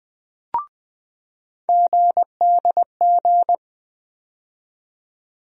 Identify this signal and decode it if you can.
Morse code, heard as a keyed tone, about 20 wpm, 700 Hz. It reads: GDG
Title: USA Call Sign Postfixes - 3 Letters 20wpm